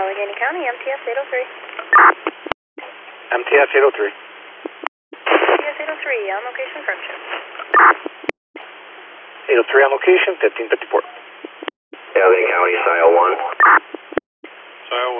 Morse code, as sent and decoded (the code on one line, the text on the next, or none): none